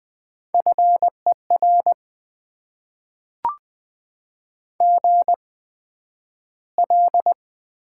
Morse code, heard as a keyed tone, about 20 wpm, spaced as for 11 wpm, 700 Hz. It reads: FER GL
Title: Top 90 QSO Elements - 8x Character Spacing - 20wpm